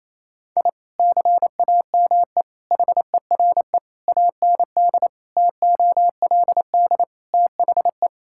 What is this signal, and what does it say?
Morse code, heard as a keyed tone, about 28 wpm, 700 Hz. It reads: I CAME HERE AND TOLD THE
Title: Sentences from Top 1000 Words - Review Part 6 - 28wpm